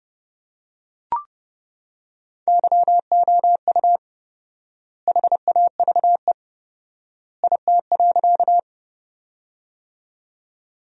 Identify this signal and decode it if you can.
Morse code, heard as a keyed tone, about 32 wpm, 700 Hz. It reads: YOU HAVE IT.
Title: 4x Word-Spacing - Sentences from Top 100 Words - 30wpm